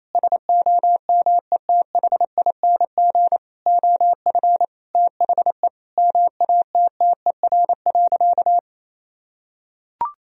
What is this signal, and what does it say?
Morse code, heard as a keyed tone, about 28 wpm, 700 Hz. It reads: SOMETHING OF THE MATTER.
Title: Sentences from Top 1000 Words - Review Part 3 - 28wpm